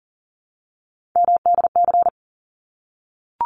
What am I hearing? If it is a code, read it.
Morse code, heard as a keyed tone, about 41 wpm, 700 Hz. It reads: MDC
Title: ARRL Sections 40wpm